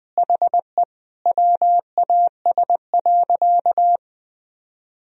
Morse code, heard as a keyed tone, about 20 wpm, 700 Hz. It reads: HE WAS.
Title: Sentences from Top 1000 Words - Review Part 1 - 20wpm